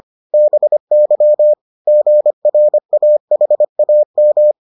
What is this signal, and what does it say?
Morse code, heard as a keyed tone, about 25 wpm, 600 Hz. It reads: BY GRAHAM